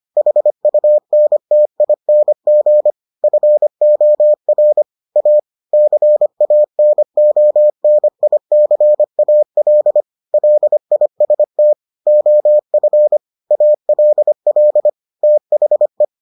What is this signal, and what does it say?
Morse code, heard as a keyed tone, about 25 wpm, 600 Hz. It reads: HUNTING FOR A CANONICAL LIST OF ALL THE